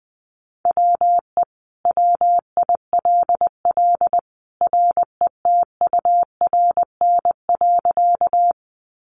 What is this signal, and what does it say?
Morse code, heard as a keyed tone, about 20 wpm, 700 Hz. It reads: WE WILL RETURN.
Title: Sentences from Top 800 Words 20wpm